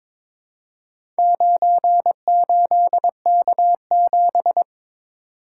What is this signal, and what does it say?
Morse code, heard as a keyed tone, about 22 wpm, 700 Hz. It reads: 98K7
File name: practice / 22wpm / Four Letter-Number Combinations - 22wpm